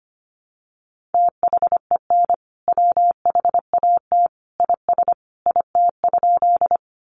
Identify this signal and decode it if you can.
Morse code, heard as a keyed tone, about 25 wpm, 700 Hz. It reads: THEN WHAT IS IT?
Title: Sentences from Top 100 Words 25wpm